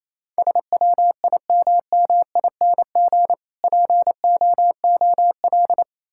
Morse code, heard as a keyed tone, about 27 wpm, 700 Hz. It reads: SWIMMING POOL
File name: Easy Words - Speed-Racing with Hints - 22wpm